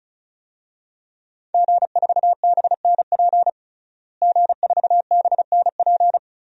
Morse code, heard as a keyed tone, about 36 wpm, 700 Hz. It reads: G4BNP G4BNP
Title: DXCC-100 Call Signs - 5 Characters - Repeated 3x - 35wpm